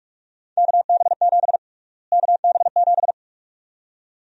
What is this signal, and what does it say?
Morse code, heard as a keyed tone, about 42 wpm, 690 Hz. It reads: KB7 KB7
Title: USA Call Sign Prefixes - Repeated 2x - 45wpm